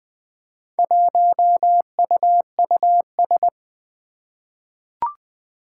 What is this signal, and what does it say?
Morse code, heard as a keyed tone, about 20 wpm, 700 Hz. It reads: 1UUS
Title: Four Challenging Letter-Number Combinations 20wpm